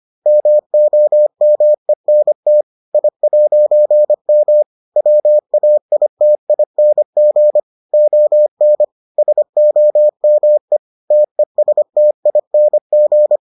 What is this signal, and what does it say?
Morse code, heard as a keyed tone, about 25 wpm, 600 Hz. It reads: MOMENT I'M WAITING ON SOME TESTING